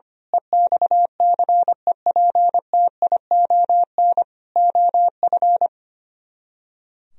Morse code, heard as a keyed tone, about 24 wpm, 700 Hz. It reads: EXCEPTION OF
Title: Speed-Racing - Sets of 4 Words 25wpm